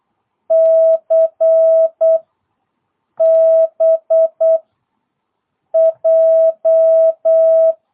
Morse code, heard as a keyed tone, about 8 wpm, 660 Hz. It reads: C B J